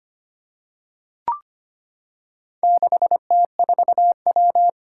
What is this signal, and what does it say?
Morse code, heard as a keyed tone, about 25 wpm, 700 Hz. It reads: 6T4W
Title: Four Challenging Letter-Number Combinations 25wpm